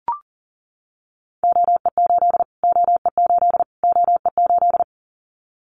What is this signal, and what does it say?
Morse code, heard as a keyed tone, about 40 wpm, 700 Hz. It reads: OE8 OE8 OE8
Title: Callsign Prefixes for Top 25 DXCC - Repeated 3x - 40wpm